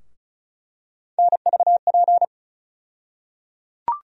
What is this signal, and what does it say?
Morse code, heard as a keyed tone, about 34 wpm, 700 Hz. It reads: NVP
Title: USA Call Sign Postfixes - 3 Letters 35wpm